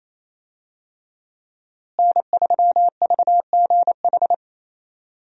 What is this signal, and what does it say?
Morse code, heard as a keyed tone, about 28 wpm, 700 Hz. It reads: N3VGH
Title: USA Call Signs - 5 Characters - Repeated 3x - 28wpm